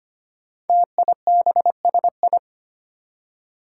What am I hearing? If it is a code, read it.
Morse code, heard as a keyed tone, about 25 wpm, 700 Hz. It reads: TIBSI
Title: Five Challenging Letter-Number Combinations 25wpm